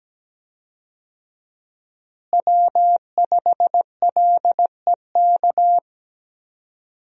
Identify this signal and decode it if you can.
Morse code, heard as a keyed tone, about 17 wpm, 700 Hz. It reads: W5LEK
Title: USA Call Signs - 5 Characters - Repeated 3x - 17wpm